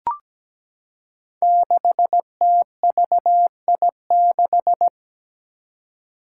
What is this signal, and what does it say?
Morse code, heard as a keyed tone, about 17 wpm, 700 Hz. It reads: 6TVI6
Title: Five Challenging Letter-Number Combinations 17wpm